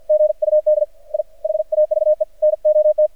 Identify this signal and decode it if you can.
Morse code, heard as a keyed tone, about 54 wpm, 610 Hz. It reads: GUD I HAVE NOT
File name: archived / CRSnet